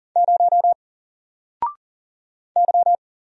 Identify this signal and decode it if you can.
Morse code, heard as a keyed tone, about 40 wpm, 700 Hz. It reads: 0 Y
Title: Single Character (Letter:Number:Punctuation:Prosign) 40wpm